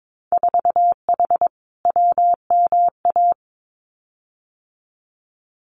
Morse code, heard as a keyed tone, about 22 wpm, 700 Hz. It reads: 4H WMA
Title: Winter Field Day Exchanges Encore 22wpm